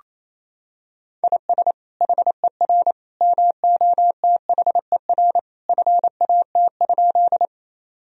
Morse code, heard as a keyed tone, about 28 wpm, 700 Hz. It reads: IS HER MOTHER FAT?
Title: Sentences from Top 1000 Words - Review Part 4 - 28wpm